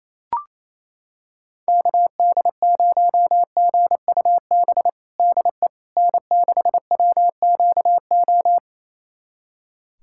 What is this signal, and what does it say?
Morse code, heard as a keyed tone, about 28 wpm, 700 Hz. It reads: KD0GUB DE N6WQO
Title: USA Callsign DE Callsign 28wpm